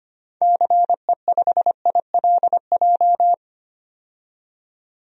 Morse code, heard as a keyed tone, about 25 wpm, 700 Hz. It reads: CE5ILJ
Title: DXCC-100 Call Signs - 6 Characters - 25wpm